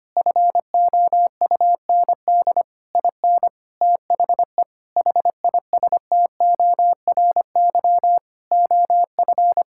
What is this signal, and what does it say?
Morse code, heard as a keyed tone, about 25 wpm, 700 Hz. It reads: FOUND IN THE HISTORY OF
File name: practice / 25wpm / Sentences with 4-Word Phrases - 25wpm